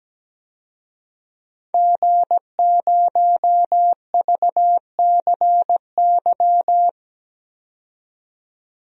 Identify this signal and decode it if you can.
Morse code, heard as a keyed tone, about 17 wpm, 700 Hz. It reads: G0VCY
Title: DXCC-100 Call Signs - 5 Characters - 17wpm